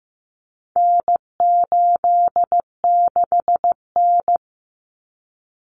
Morse code, heard as a keyed tone, about 15 wpm, 700 Hz. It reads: N86N
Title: Common Cut Numbers - 4-Digit 15wpm